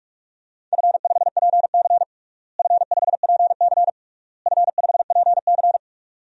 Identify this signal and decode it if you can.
Morse code, heard as a keyed tone, about 46 wpm, 700 Hz. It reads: F5PC F5PC F5PC
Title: DXCC-100 Call Signs - 4 Characters - Repeated 3x - 45wpm